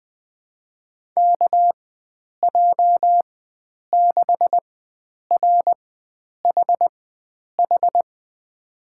Morse code, heard as a keyed tone, about 20 wpm, spaced as for 9 wpm, 700 Hz. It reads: KJ6RHH
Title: USA Call Signs - 6 Characters - 4x Character Spacing - 20wpm